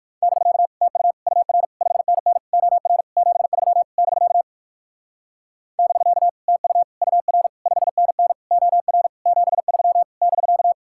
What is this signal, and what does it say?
Morse code, heard as a keyed tone, about 57 wpm, 700 Hz. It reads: <BK> TU UR 5NN OR 73 <BK> <BK> TU UR 5NN OR 73 <BK>